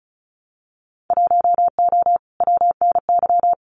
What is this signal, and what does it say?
Morse code, heard as a keyed tone, about 36 wpm, 700 Hz. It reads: 1O WNY
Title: Winter Field Day Exchanges 35wpm